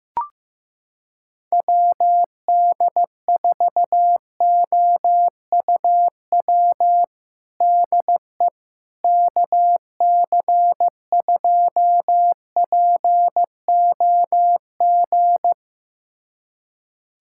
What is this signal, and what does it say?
Morse code, heard as a keyed tone, about 15 wpm, 700 Hz. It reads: WD4OUW DE KC2POG
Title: USA Callsign DE Callsign 15wpm